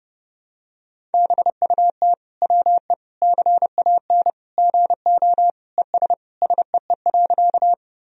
Morse code, heard as a keyed tone, about 30 wpm, 700 Hz. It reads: BUT WE CAN GO ES SEE.
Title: Sentences from Top 100 Words and Abbreviations 30wpm